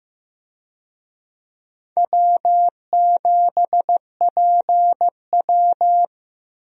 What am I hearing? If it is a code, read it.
Morse code, heard as a keyed tone, about 15 wpm, 700 Hz. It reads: W7PW
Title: USA Call Signs - 4 Characters - Repeated 3x - 15wpm